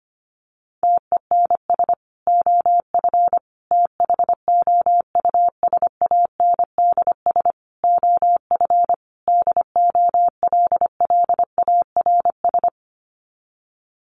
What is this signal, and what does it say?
Morse code, heard as a keyed tone, about 25 wpm, 700 Hz. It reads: TENS OF THOUSANDS OF DOLLARS
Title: Sets of 5 Words Encore2 25wpm